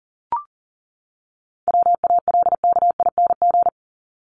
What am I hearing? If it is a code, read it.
Morse code, heard as a keyed tone, about 40 wpm, 700 Hz. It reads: WALKING
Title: ING Suffix - Top 100 Words with Base Words 40wpm